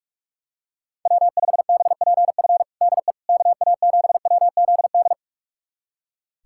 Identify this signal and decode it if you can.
Morse code, heard as a keyed tone, about 46 wpm, 700 Hz. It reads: W5BPF DE KA7WZD
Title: USA Callsign DE Callsign 45wpm